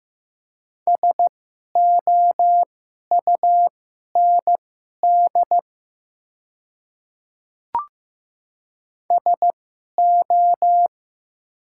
Morse code, heard as a keyed tone, about 15 wpm, spaced as for 11 wpm, 700 Hz. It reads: SOUND SO
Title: 2x Character Spacing - Top 100 Words - 15wpm